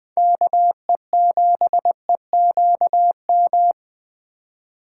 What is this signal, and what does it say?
Morse code, heard as a keyed tone, about 20 wpm, 700 Hz. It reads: KE7EQM